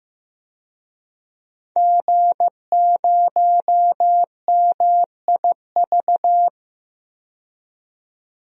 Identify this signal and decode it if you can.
Morse code, heard as a keyed tone, about 15 wpm, 700 Hz. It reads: G0MIV
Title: DXCC-100 Call Signs - 5 Characters - 15wpm